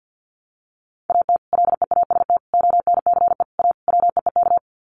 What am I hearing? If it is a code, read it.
Morse code, heard as a keyed tone, about 50 wpm, 700 Hz. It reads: AT LEAST ONCE A WEEK
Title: Sets of 5 Words Encore2 50wpm